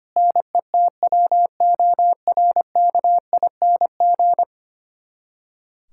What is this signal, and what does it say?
Morse code, heard as a keyed tone, about 25 wpm, 700 Hz. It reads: NETWORKING
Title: K1USN SST QSO Practice 25wpm